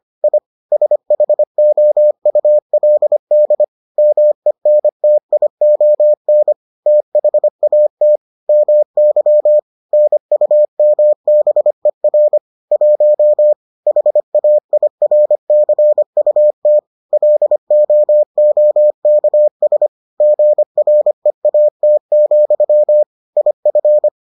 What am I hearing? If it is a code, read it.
Morse code, heard as a keyed tone, about 25 wpm, 600 Hz. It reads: I SHOULD MENTION THAT MY NUMBER 1 HAIRCUT LOOKS GREAT, IF